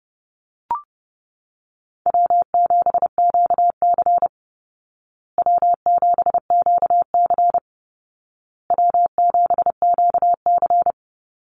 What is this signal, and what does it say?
Morse code, heard as a keyed tone, about 29 wpm, 700 Hz. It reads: W7QC W7QC W7QC
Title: Easy Call Signs Repeated 3x - 30wpm